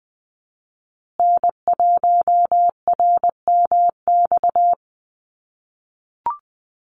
Morse code, heard as a keyed tone, about 20 wpm, 700 Hz. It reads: N1RMX
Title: USA Call Signs Repeated 2x 20wpm